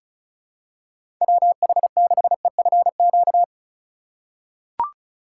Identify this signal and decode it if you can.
Morse code, heard as a keyed tone, about 36 wpm, 700 Hz. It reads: WH6EFQ
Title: USA Call Signs - 2x Character Spacing - 35wpm